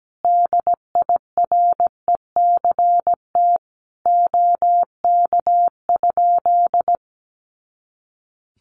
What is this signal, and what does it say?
Morse code, heard as a keyed tone, about 17 wpm, 700 Hz. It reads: DIRECT OK?
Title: Common QSO Phrases 17wpm